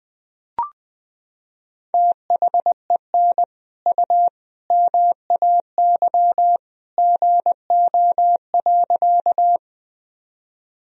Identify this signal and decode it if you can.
Morse code, heard as a keyed tone, about 21 wpm, 700 Hz. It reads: THEN U MAY GO.
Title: Sentences from Top 100 Words and Abbreviations 20wpm